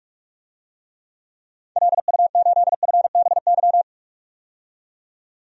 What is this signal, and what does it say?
Morse code, heard as a keyed tone, about 44 wpm, 700 Hz. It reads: LU8FBY
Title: DXCC-100 Call Signs - 6 Characters - 45wpm